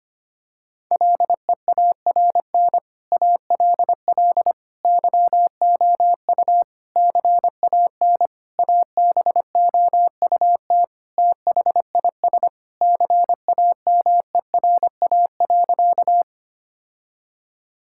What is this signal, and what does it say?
Morse code, heard as a keyed tone, about 25 wpm, 700 Hz. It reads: LEARN ALL YOU CAN ABOUT THIS CAMERA.
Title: Sentences from Top 1000 Words - Review Part 5 - 25wpm